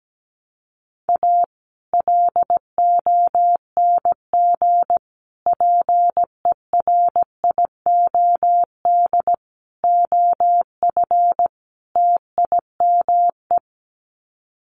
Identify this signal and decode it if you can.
Morse code, heard as a keyed tone, about 17 wpm, 700 Hz. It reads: A LONG PERIOD OF TIME
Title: Sets of 5 Words 17wpm